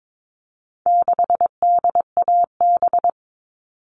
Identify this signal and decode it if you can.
Morse code, heard as a keyed tone, about 22 wpm, 700 Hz. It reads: 6DAB
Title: All Cut Numbers - 4-Digit 22wpm